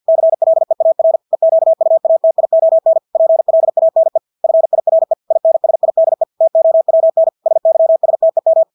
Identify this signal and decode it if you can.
Morse code, heard as a keyed tone, about 50 wpm, 650 Hz. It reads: CLEAR EQUATION PLANE FIRE INSIDE TOWN SYSTEM
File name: Engelske ord 50 wpm